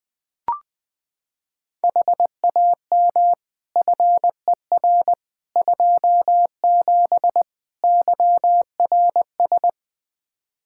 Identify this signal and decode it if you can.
Morse code, heard as a keyed tone, about 21 wpm, 700 Hz. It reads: HAM FER 27 YRS
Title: Practice QSO - Exchange 3 - More Info - 20wpm